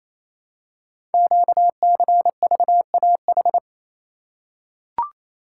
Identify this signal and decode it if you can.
Morse code, heard as a keyed tone, about 27 wpm, 700 Hz. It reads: QCVAH
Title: Code Groups - 5 Letters - 28wpm